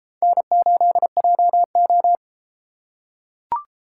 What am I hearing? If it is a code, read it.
Morse code, heard as a keyed tone, about 31 wpm, 700 Hz. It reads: N8JO